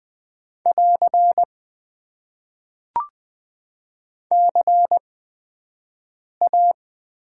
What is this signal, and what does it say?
Morse code, heard as a keyed tone, about 20 wpm, spaced as for 5 wpm, 700 Hz. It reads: <AR> CA
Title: Top 90 QSO Elements - 8x Character Spacing - 20wpm